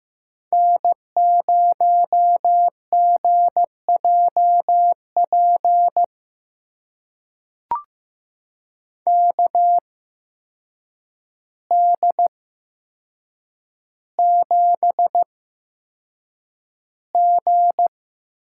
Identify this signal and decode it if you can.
Morse code, heard as a keyed tone, about 15 wpm, spaced as for 7 wpm, 700 Hz. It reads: N0GJP KD7G
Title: USA Call Signs - 8x Character Spacing - 15wpm